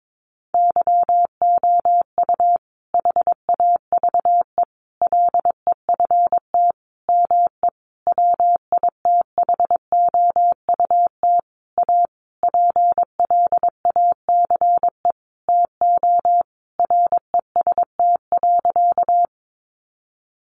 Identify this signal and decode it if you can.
Morse code, heard as a keyed tone, about 22 wpm, 700 Hz. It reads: YOU HAVE LEFT ME WITHOUT A PLACE TO REST.
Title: Sentences from Top 600 Words 22wpm